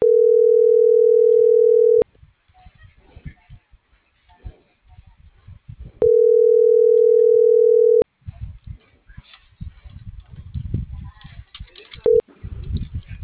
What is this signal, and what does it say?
Ringback tone (North America), 440 + 480 Hz.